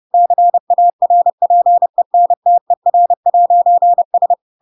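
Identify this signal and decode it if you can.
Morse code, heard as a keyed tone, about 30 wpm, 700 Hz. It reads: CARPENTER'S
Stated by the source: Chapter 2 of The Adventures of Pinocchio read as a book in Morse code (CW) at 30 wpm, by ik1hge.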